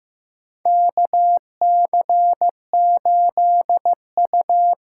Morse code, heard as a keyed tone, about 15 wpm, 700 Hz. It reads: KC8U
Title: USA Call Signs - 6 Characters - 8x Character Spacing - 15wpm